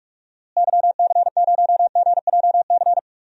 Morse code, heard as a keyed tone, about 44 wpm, 700 Hz. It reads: YC0GJC
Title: DXCC-100 Call Signs - 6 Characters - Repeated 3x - 45wpm